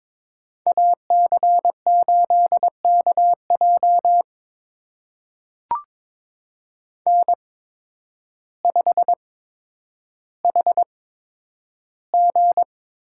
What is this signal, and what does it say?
Morse code, heard as a keyed tone, about 22 wpm, spaced as for 10 wpm, 700 Hz. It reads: AC8KJ N5HG
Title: USA Call Signs - 5 Characters - 8x Character Spacing - 22wpm